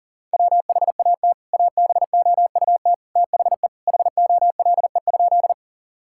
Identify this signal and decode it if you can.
Morse code, heard as a keyed tone, about 40 wpm, 700 Hz. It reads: WHAT ABOUT THE HOLE?
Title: Sentences from Top 1600 Words 40wpm